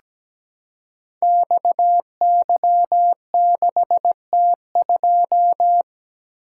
Morse code, heard as a keyed tone, about 17 wpm, 700 Hz. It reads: XY6T2
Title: Code Groups - 5 Letter:Numbers - 17wpm